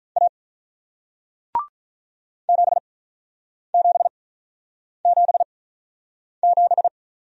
Morse code, heard as a keyed tone, about 47 wpm, spaced as for 9 wpm, 700 Hz. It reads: A 7777